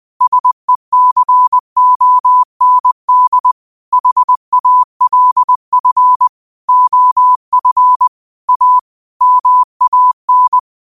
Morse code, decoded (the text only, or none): SECOND HALF OF A MAN